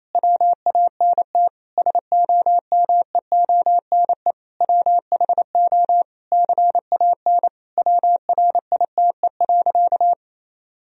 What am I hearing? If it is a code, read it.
Morse code, heard as a keyed tone, about 28 wpm, 700 Hz. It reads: WANT SOMEONE WHO CAN WRITE.
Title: Sentences from Top 1000 Words - Review Part 1 - 28wpm